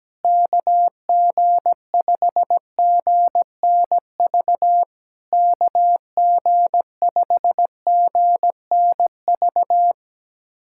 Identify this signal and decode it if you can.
Morse code, heard as a keyed tone, about 17 wpm, 700 Hz. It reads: KG5GNV KG5GNV
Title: SOTA QSO Practice 17wpm